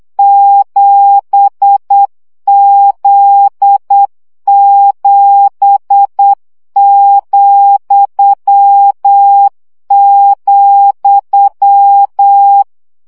Morse code, heard as a keyed tone, about 8 wpm, 800 Hz. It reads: 7Z7,,
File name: CW gyakorlas z 7 vesszo 7 baud